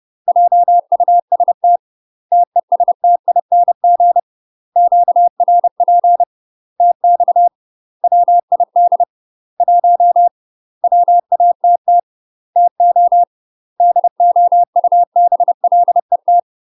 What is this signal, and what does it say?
Morse code, heard as a keyed tone, about 30 wpm, 700 Hz. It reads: JUST TESTING QRP TX WID 1 WATT TO DOUBLET
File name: UK MASTER QSOs TRAINING/30WPM